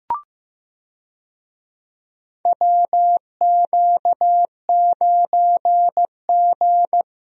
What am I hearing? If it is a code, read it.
Morse code, heard as a keyed tone, about 15 wpm, 700 Hz. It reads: WQ9G
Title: USA Call Signs - 4 Characters - Repeated 3x - 15wpm